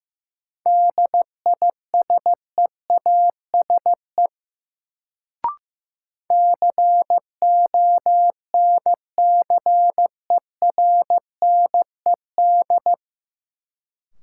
Morse code, heard as a keyed tone, about 15 wpm, 700 Hz. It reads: DISEASE CONCERNED
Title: Sentences from Top 1000 Words - Review Part 6 - New Word Drill - 15wpm